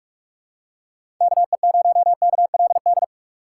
Morse code, heard as a keyed tone, about 43 wpm, 700 Hz. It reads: KE0KLD
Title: USA Call Signs - 6 Characters - 4x Character Spacing - 45wpm